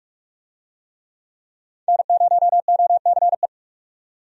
Morse code, heard as a keyed tone, about 42 wpm, 700 Hz. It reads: N0OCE
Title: USA Call Signs - 5 Characters - 45wpm